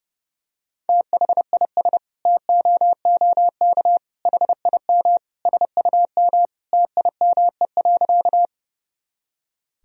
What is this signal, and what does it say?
Morse code, heard as a keyed tone, about 30 wpm, 700 Hz. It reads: THIS TOOK HIM SUM TIME.
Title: Sentences from Top 300 Words and Abbreviations 30wpm